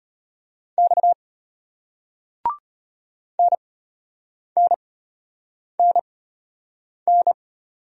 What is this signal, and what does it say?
Morse code, heard as a keyed tone, about 34 wpm, spaced as for 8 wpm, 700 Hz. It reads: X NNNN